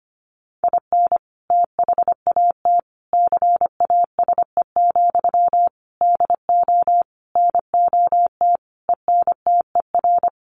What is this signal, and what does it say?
Morse code, heard as a keyed tone, about 25 wpm, 700 Hz. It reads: IN THAT CASE, DO NOT ENTER
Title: Sentences from Top 1300 Words 25wpm